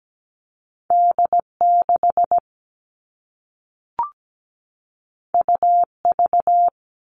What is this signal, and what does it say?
Morse code, heard as a keyed tone, about 17 wpm, 700 Hz. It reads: D6 UV
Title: All Cut Numbers - 2-Digit 17wpm